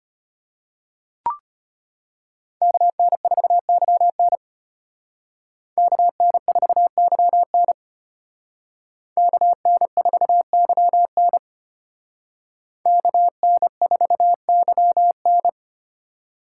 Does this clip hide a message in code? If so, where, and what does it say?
Yes: Morse code from 2.6 to 15.5 s, heard as a keyed tone, about 32 wpm, 700 Hz. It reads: KN4YN KN4YN KN4YN KN4YN